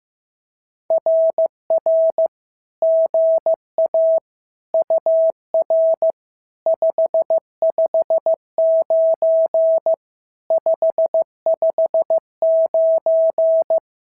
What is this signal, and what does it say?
Morse code, heard as a keyed tone, about 15 wpm, 650 Hz. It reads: RR GA UR 559 559
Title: SOTA QSO Practice 15wpm